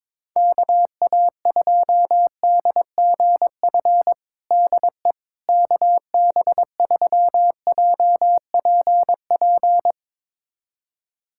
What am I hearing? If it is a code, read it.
Morse code, heard as a keyed tone, about 22 wpm, 700 Hz. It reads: KA2DGF DE KB3JPP
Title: USA Callsign DE Callsign - 6 Characters - 22wpm